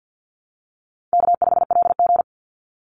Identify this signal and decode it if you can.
Morse code, heard as a keyed tone, about 50 wpm, 700 Hz. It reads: K5LG
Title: Call Signs - 50wpm